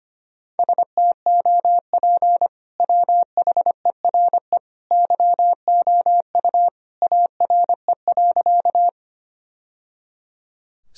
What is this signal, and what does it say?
Morse code, heard as a keyed tone, about 25 wpm, 700 Hz. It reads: STOP WHERE YOU ARE.
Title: Sentences from Top 1000 Words - Review Part 1 - 25wpm